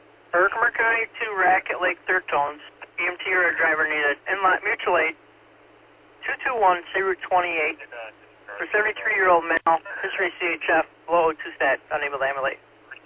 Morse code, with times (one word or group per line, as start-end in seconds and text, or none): none